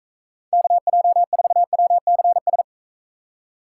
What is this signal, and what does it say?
Morse code, heard as a keyed tone, about 40 wpm, 700 Hz. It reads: KJ4WCS